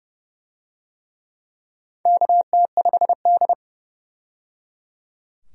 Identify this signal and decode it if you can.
Morse code, heard as a keyed tone, about 29 wpm, 700 Hz. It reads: KT5D
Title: USA Call Signs - 4 Characters - 30wpm